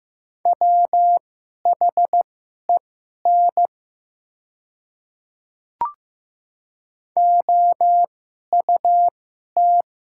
Morse code, heard as a keyed tone, about 15 wpm, spaced as for 11 wpm, 700 Hz. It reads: WHEN OUT
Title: 2x Character Spacing - Top 100 Words - 15wpm